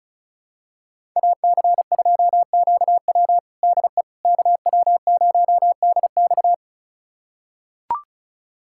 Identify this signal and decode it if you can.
Morse code, heard as a keyed tone, about 36 wpm, 700 Hz. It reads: AC2QW DE KW0DX
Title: USA Callsign DE Callsign - 5 Characters - 35wpm